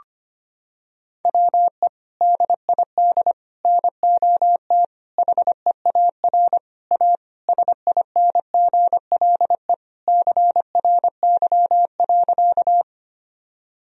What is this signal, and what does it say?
Morse code, heard as a keyed tone, about 25 wpm, 700 Hz. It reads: WE DID NOT HEAR A SINGLE CRY.
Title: Sentences from Top 1000 Words - Review Part 4 - 25wpm